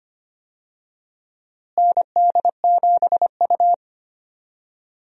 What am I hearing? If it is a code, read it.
Morse code, heard as a keyed tone, about 25 wpm, 700 Hz. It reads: ND7U
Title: USA Call Signs - 4 Characters - Repeated 3x - 25wpm